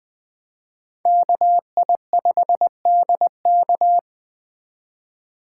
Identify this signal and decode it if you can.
Morse code, heard as a keyed tone, about 20 wpm, 700 Hz. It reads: KI5DK
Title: Speed Racing - USA Call Signs 17wpm